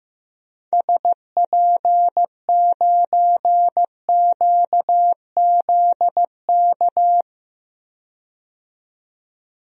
Morse code, heard as a keyed tone, about 16 wpm, 700 Hz. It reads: SP9QZK
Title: DXCC-100 Call Signs - 6 Characters - 15wpm